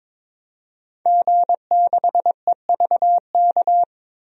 Morse code, heard as a keyed tone, about 21 wpm, 700 Hz. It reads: G6EVK
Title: DXCC-100 Call Signs - 5 Characters - 22wpm